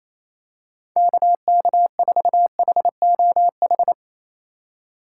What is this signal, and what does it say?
Morse code, heard as a keyed tone, about 28 wpm, 700 Hz. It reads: KK4HOH
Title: USA Call Signs - 6 Characters - 2x Character Spacing - 28wpm